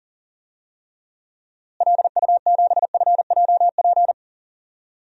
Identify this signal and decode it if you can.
Morse code, heard as a keyed tone, about 40 wpm, 700 Hz. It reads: LU7FJP
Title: DXCC-100 Call Signs - 6 Characters - Repeated 3x - 40wpm